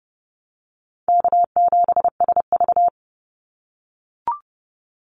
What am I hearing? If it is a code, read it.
Morse code, heard as a keyed tone, about 30 wpm, 700 Hz. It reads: K7SV
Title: Call Signs - 30wpm